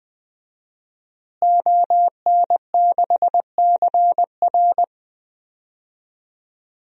Morse code, heard as a keyed tone, about 20 wpm, 700 Hz. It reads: ON6CR
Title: DXCC-100 Call Signs - 5 Characters - Repeated 2x - 20wpm